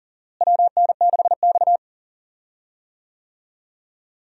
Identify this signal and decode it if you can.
Morse code, heard as a keyed tone, about 40 wpm, 700 Hz. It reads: WN6X
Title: USA Call Signs - 4 Characters - Repeated 3x - 40wpm